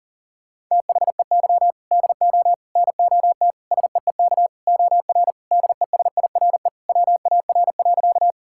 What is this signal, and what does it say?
Morse code, heard as a keyed tone, about 40 wpm, 700 Hz. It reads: THEY DO NOT SEEK OR DESIRE WAR.
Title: Sentences from Top 1600 Words 40wpm